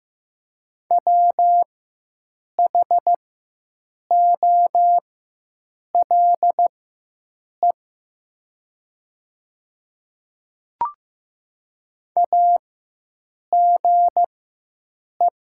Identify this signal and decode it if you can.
Morse code, heard as a keyed tone, about 15 wpm, spaced as for 7 wpm, 700 Hz. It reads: WHOLE AGE
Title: 4x Character Spacing - Top 500 Words - 15wpm